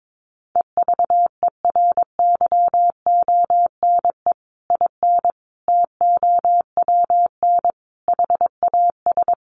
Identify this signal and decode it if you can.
Morse code, heard as a keyed tone, about 22 wpm, 700 Hz. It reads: EVERYONE IN TOWN HAS
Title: Sentences from Top 1500 Words 22wpm